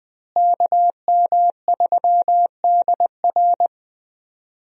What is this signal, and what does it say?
Morse code, heard as a keyed tone, about 20 wpm, 700 Hz. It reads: KM3DR
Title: USA Call Signs - 5 Characters - Repeated 3x - 20wpm